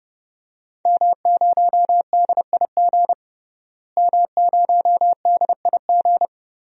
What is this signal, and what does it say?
Morse code, heard as a keyed tone, about 32 wpm, 700 Hz. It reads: M0DIG M0DIG
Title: DXCC-100 Call Signs - 5 Characters - Repeated 3x - 30wpm